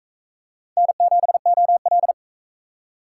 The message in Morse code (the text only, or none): N7OL